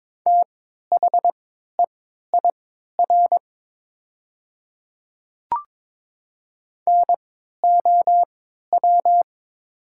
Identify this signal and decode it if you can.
Morse code, heard as a keyed tone, about 22 wpm, spaced as for 13 wpm, 700 Hz. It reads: THEIR NOW
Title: 3x Character Spacing - Top 100 Words - 22wpm